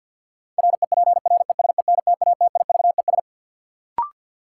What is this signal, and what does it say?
Morse code, heard as a keyed tone, about 53 wpm, 700 Hz. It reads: REPRESENTATIVES
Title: Sets of 5 Words Encore - New Word Drill - 50wpm